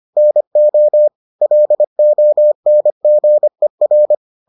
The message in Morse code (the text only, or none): NO LONGER